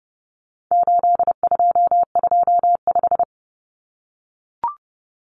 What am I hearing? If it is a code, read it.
Morse code, heard as a keyed tone, about 30 wpm, 700 Hz. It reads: 8225
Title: Numbers - 4-Digit 30wpm